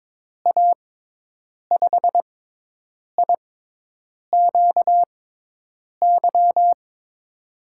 Morse code, heard as a keyed tone, about 23 wpm, spaced as for 8 wpm, 700 Hz. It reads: A5IQY
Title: USA Call Signs - 6 Characters - 6x Character Spacing - 22wpm